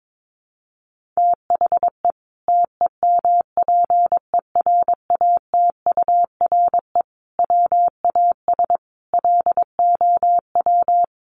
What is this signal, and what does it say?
Morse code, heard as a keyed tone, about 22 wpm, 700 Hz. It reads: THE TEMPERATURE WAS LOW
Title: Sentences from Top 1500 Words 22wpm